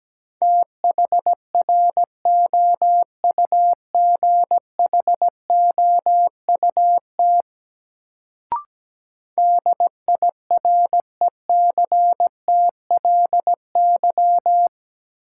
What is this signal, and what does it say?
Morse code, heard as a keyed tone, about 17 wpm, 700 Hz. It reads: THROUGHOUT DIRECTLY
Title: Sentences from Top 800 Words - New Word Drill - 17wpm